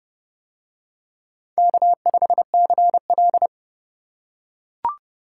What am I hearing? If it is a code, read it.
Morse code, heard as a keyed tone, about 29 wpm, 700 Hz. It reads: K5CL